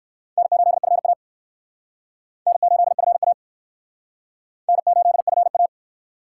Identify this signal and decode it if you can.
Morse code, heard as a keyed tone, about 60 wpm, 700 Hz. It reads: N8FA N8FA N8FA